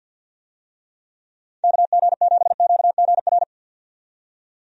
Morse code, heard as a keyed tone, about 48 wpm, 700 Hz. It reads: KG7QGR
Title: USA Call Signs - 6 Characters - 50wpm